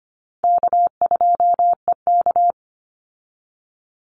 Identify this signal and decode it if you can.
Morse code, heard as a keyed tone, about 25 wpm, 700 Hz. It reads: K2EK
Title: Easy Call Signs - 25wpm